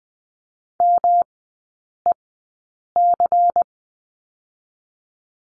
Morse code, heard as a keyed tone, about 20 wpm, spaced as for 8 wpm, 700 Hz. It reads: MEC
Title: 152 Lesson 25e - Letter p - Three Characters 20wpm